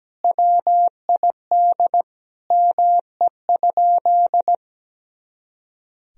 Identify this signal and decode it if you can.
Morse code, heard as a keyed tone, about 17 wpm, 700 Hz. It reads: WID ME?
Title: Sentences from Top 300 Words and Abbreviations 17wpm